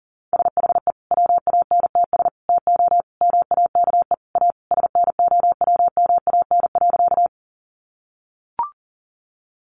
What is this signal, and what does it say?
Morse code, heard as a keyed tone, about 37 wpm, 700 Hz. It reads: SHE WANTS TO MAKE A SNOWMAN.
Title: Easy Phrases 40wpm